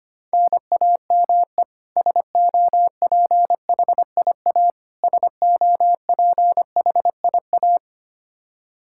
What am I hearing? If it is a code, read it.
Morse code, heard as a keyed tone, about 25 wpm, 700 Hz. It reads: NAME SOPHIA SOPHIA
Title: Practice QSO - Exchange 1 - Beginning - 25wpm